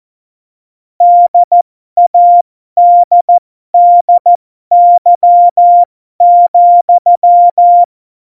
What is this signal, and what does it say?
Morse code, heard as a keyed tone, about 14 wpm, spaced as for 12 wpm, 700 Hz. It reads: DADDY,
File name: Morsekurs / Witze